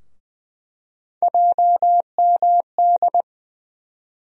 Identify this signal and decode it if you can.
Morse code, heard as a keyed tone, about 20 wpm, 700 Hz. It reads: JMD
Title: USA Call Sign Postfixes - 3 Letters 20wpm